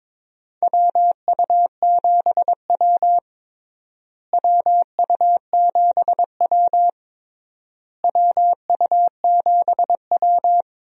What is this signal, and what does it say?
Morse code, heard as a keyed tone, about 22 wpm, 700 Hz. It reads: WU7W WU7W WU7W
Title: USA Call Signs - 4 Characters - Repeated 3x - 22wpm